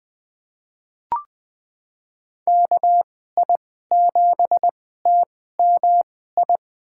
Morse code, heard as a keyed tone, about 20 wpm, spaced as for 14 wpm, 700 Hz. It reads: KI7TMI
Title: USA Call Signs - 6 Characters - 2x Character Spacing - 20wpm